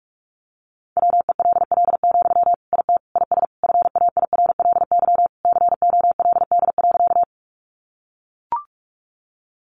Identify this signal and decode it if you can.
Morse code, heard as a keyed tone, about 44 wpm, 700 Hz. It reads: WELL, IT IS FAIRLY COLD.
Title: Sentences from Top 1100 Words 45wpm